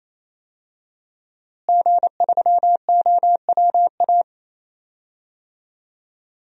Morse code, heard as a keyed tone, about 28 wpm, 700 Hz. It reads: G3OWA
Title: DXCC-100 Call Signs - 5 Characters - Repeated 3x - 28wpm